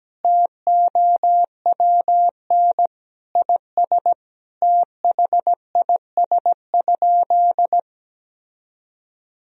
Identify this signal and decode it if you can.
Morse code, heard as a keyed tone, about 17 wpm, 700 Hz. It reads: TOWN IS THIS?
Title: Sentences from Top 1000 Words - Review Part 3 - 17wpm